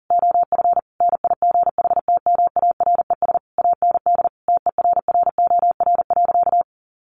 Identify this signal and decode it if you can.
Morse code, heard as a keyed tone, about 40 wpm, 700 Hz. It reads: OF NIGHTMARES AND TERROR.
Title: A Sound Of Thunder 40wpm